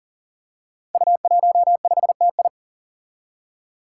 Morse code, heard as a keyed tone, about 41 wpm, 700 Hz. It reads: U15TI
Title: Five Challenging Letter-Number Combinations 40wpm